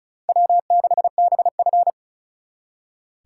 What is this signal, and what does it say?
Morse code, heard as a keyed tone, about 46 wpm, 700 Hz. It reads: W6BF